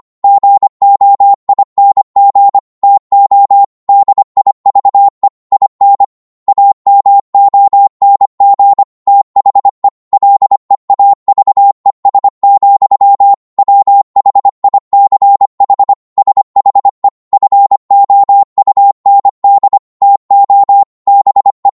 Morse code, heard as a keyed tone, about 25 wpm, 800 Hz. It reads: GOING TO DIVE IN AMONG THE LEAVES, WHICH SHE FOUND TO BE
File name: Morse / psg-25wpm